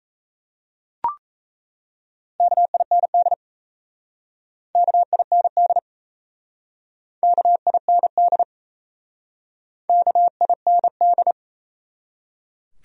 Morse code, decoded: KIND KIND KIND KIND

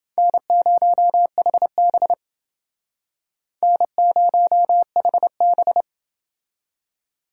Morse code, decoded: N0HB N0HB

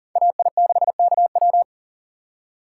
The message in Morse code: AI6KW

